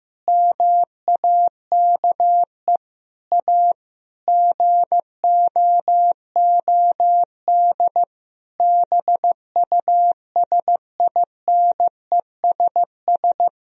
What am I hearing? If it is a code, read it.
Morse code, heard as a keyed tone, about 15 wpm, 700 Hz. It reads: MAKE A GOOD BUSINESS
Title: Sentences from Top 1000 Words - Review Part 4 - 15wpm